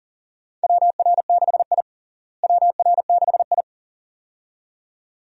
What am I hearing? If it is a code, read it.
Morse code, heard as a keyed tone, about 40 wpm, 700 Hz. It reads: WR6I WR6I
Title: USA Call Signs - 4 Characters - Repeated 2x - 40wpm